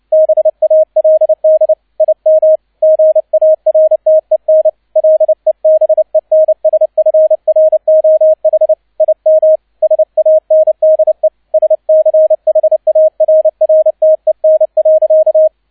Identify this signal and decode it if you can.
Morse code, heard as a keyed tone, about 29 wpm, 630 Hz, with radio noise. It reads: BALD IM GARTEN LEBENSFROH IM SANDE SCHARRTEN.
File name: 002 Zweiter Streich 30wpm